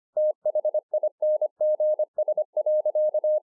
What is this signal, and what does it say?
Morse code, heard as a keyed tone, about 25 wpm, 610 Hz. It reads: THINGS.
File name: 2011-04-mp3 / THIS IS NOT MY SHACK 2 25WPM